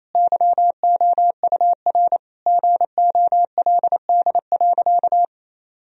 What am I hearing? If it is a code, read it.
Morse code, heard as a keyed tone, about 28 wpm, 700 Hz. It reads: YOUR GOLD.
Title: Sentences from Top 1000 Words - Review Part 3 - 28wpm